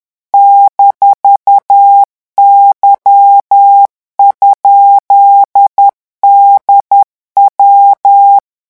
Morse code, heard as a keyed tone, about 10 wpm, 800 Hz. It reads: -Y?DW
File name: morse mp3 / revision17-vitesse 10 mots